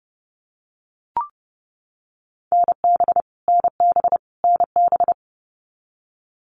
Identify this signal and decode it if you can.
Morse code, heard as a keyed tone, about 30 wpm, 700 Hz. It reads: NB NB NB
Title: Canadian Provinces and Territories - Repeated 3x - 30wpm